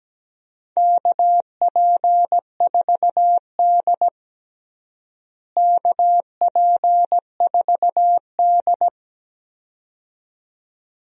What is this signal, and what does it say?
Morse code, heard as a keyed tone, about 17 wpm, 700 Hz. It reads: KP4D KP4D
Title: DXCC-100 Call Signs - 4 Characters - Repeated 2x - 17wpm